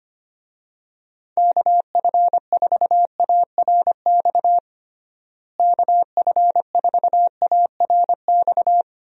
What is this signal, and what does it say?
Morse code, heard as a keyed tone, about 25 wpm, 700 Hz. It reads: KF4ARX KF4ARX
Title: USA Call Signs - 6 Characters - Repeated 2x - 25wpm